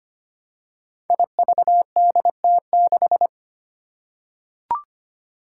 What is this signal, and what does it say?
Morse code, heard as a keyed tone, about 25 wpm, 700 Hz. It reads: IVDT6
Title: Five Challenging Letter-Number Combinations 25wpm